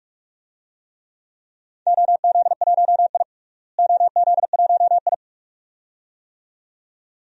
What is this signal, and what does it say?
Morse code, heard as a keyed tone, about 46 wpm, 700 Hz. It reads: OZ1I OZ1I
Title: DXCC-100 Call Signs - 4 Characters - Repeated 2x - 45wpm